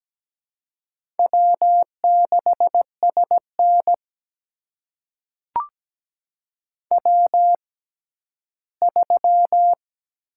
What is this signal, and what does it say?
Morse code, heard as a keyed tone, about 17 wpm, spaced as for 11 wpm, 700 Hz. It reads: W6SN W3
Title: USA Call Signs - 4 Characters - 6x Character Spacing - 17wpm